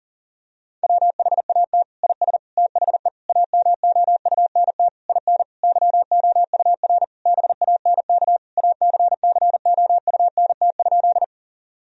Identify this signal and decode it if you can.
Morse code, heard as a keyed tone, about 37 wpm, 700 Hz. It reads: WHAT IS THE AMOUNT IN YOUR BANK ACCOUNT?
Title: Sentences from Top 1000 Words - Review Part 4 - 40wpm